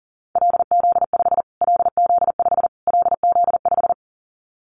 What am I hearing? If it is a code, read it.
Morse code, heard as a keyed tone, about 40 wpm, 700 Hz. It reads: LZ5 LZ5 LZ5
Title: Callsign Prefixes for Top 25 DXCC - Repeated 3x - 40wpm